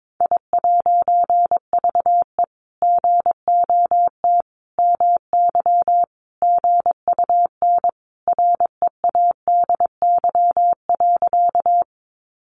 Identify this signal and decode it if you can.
Morse code, heard as a keyed tone, about 22 wpm, 700 Hz. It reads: I'VE GOT MY GUN READY.
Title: A Sound Of Thunder 22wpm